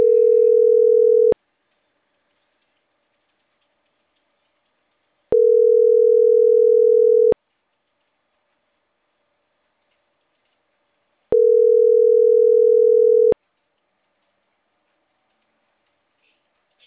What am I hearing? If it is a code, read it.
Ringback tone (North America), 440 + 480 Hz.
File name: CENTRAL CAEA / out agcaea10